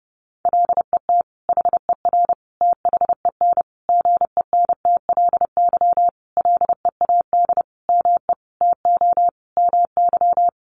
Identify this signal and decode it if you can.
Morse code, heard as a keyed tone, about 30 wpm, 700 Hz. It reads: LET HER THEN GENTLY LEAD ME TO MY
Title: Sentences from Top 1500 Words 30wpm